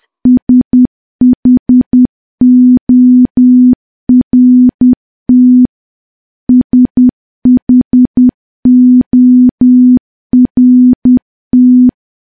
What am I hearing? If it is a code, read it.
Morse code, heard as a keyed tone, about 10 wpm, 250 Hz. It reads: SHORT SHORT
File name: morse / morse-training-10WPM